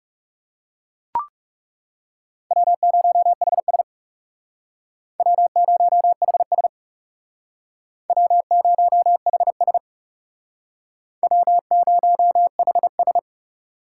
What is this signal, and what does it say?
Morse code, heard as a keyed tone, about 36 wpm, 700 Hz. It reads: W0HS W0HS W0HS W0HS
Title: Speed Racing - USA 1x2 Call Signs 30wpm